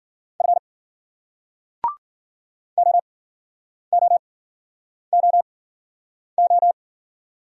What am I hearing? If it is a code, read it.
Morse code, heard as a keyed tone, about 51 wpm, spaced as for 9 wpm, 700 Hz. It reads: F OOOO